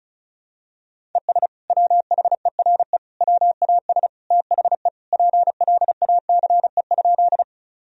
Morse code, heard as a keyed tone, about 34 wpm, 700 Hz. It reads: ES WHERE WAS THE PLACE?
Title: Sentences from Top 200 Words and Abbreviations 35wpm